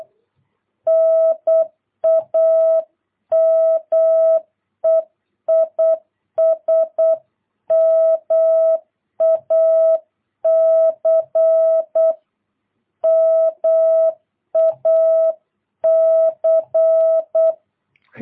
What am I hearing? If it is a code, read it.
Morse code, heard as a keyed tone, about 8 wpm, 660 Hz. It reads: NAME IS MAC MAC
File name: jh8cbh / 2022cw